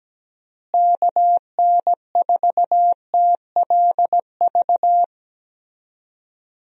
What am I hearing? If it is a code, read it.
Morse code, heard as a keyed tone, about 17 wpm, 700 Hz. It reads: KN4TLV
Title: USA Call Signs - 6 Characters - Repeated 2x - 17wpm